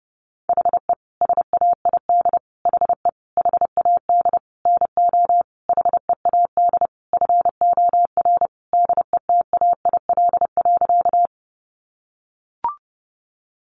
Morse code, heard as a keyed tone, about 30 wpm, 700 Hz. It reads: HE SAID HE HAD NO HEAD FOR DETAIL.
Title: Sentences from Top 1500 Words 30wpm